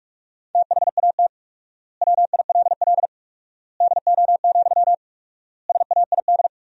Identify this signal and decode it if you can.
Morse code, heard as a keyed tone, about 46 wpm, 700 Hz. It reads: THAT WILL DO, SAID
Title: 4x Word-Spacing - Sentences from Top 100 Words - 45wpm